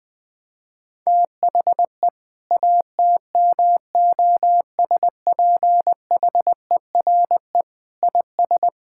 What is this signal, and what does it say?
Morse code, heard as a keyed tone, about 21 wpm, 700 Hz. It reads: THE ATMOSPHERE IS
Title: Sentences from Top 1600 Words 20wpm